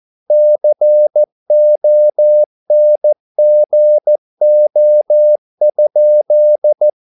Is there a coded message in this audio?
Yes: Morse code from 0.3 to 6.9 s, heard as a keyed tone, about 14 wpm, 600 Hz. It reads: CONGO?